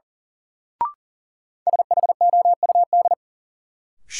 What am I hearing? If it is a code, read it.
Morse code, heard as a keyed tone, about 37 wpm, 700 Hz. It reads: SHOUD
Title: Sentences from Top 1000 Words - Review Part 4 - New Word Drill - 40wpm